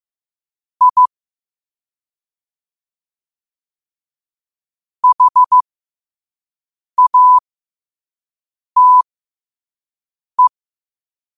Morse code, heard as a keyed tone, about 15 wpm, spaced as for 5 wpm, 1000 Hz. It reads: I HATE
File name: 5wpm